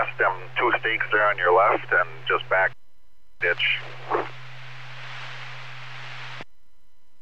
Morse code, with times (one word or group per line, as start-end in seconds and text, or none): none